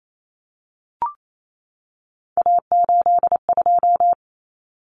Morse code, heard as a keyed tone, about 28 wpm, 700 Hz. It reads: A82
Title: Common Cut Numbers - 3-Digit 28wpm